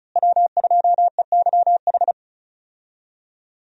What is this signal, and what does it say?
Morse code, heard as a keyed tone, about 36 wpm, 700 Hz. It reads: W2EYH
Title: USA Call Signs - 5 Characters - Repeated 2x - 35wpm